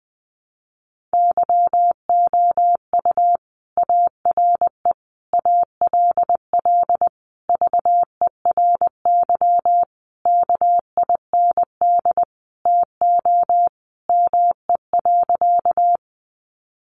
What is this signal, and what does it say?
Morse code, heard as a keyed tone, about 19 wpm, 700 Hz. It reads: YOU ARE ALL VERY KIND TO ME.
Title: Sentences from Top 200 Words 20wpm